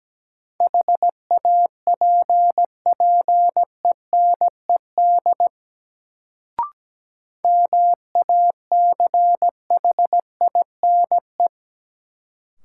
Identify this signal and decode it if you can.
Morse code, heard as a keyed tone, about 17 wpm, 700 Hz. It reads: HAPPENED MACHINE
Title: Sentences from Top 1000 Words - Review Part 2 - New Word Drill - 17wpm